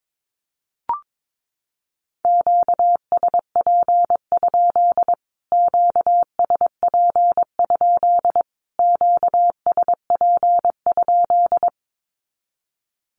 Morse code, heard as a keyed tone, about 22 wpm, 700 Hz. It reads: QSP? QSP? QSP?
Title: Q-Codes Repeated 3x 22wpm